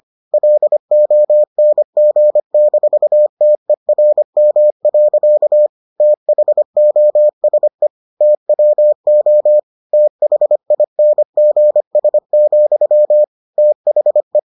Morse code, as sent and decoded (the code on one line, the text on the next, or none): .-.. --- -. --. -....- - . .-. -- .-.-.- / - .... --- ... / - .-- --- / - .... .. -. --. ... --..-- / - .... .
LONG-TERM. THOSE TWO THINGS, THE